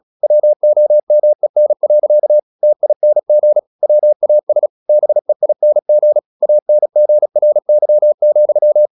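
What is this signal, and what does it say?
Morse code, heard as a keyed tone, about 36 wpm, 600 Hz. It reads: WOMEN. TING WAS BEING ANGRY,
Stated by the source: Morse practice at 36 wpm, text of Ken Follett's novel Never, from CW med Gnister.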